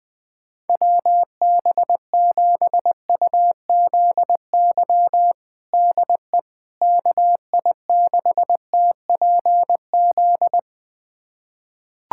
Morse code, heard as a keyed tone, about 21 wpm, 700 Hz. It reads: WB7UZY DE KI6TPZ
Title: USA Callsign DE Callsign - 6 Characters - 20wpm